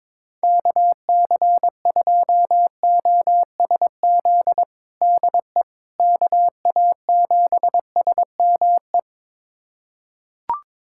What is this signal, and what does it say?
Morse code, heard as a keyed tone, about 22 wpm, 700 Hz. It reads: KC2OSZ DE KA7SME
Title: USA Callsign DE Callsign - 6 Characters - 22wpm